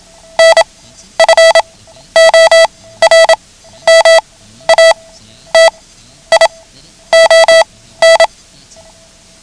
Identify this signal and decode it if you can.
Morse code, heard as a keyed tone, about 27 wpm, spaced as for 12 wpm, 660 Hz. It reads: NFORMATION
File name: ham radio / CodeQuick / Code Quick 09